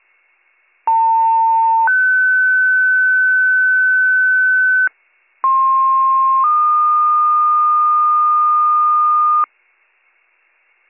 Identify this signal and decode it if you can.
Two-tone page, heard as a one-second tone, then a three-second tone, about 900 then 1500 Hz, about 1060 then 1220 Hz. It is heard from 0.9 to 9.4 s.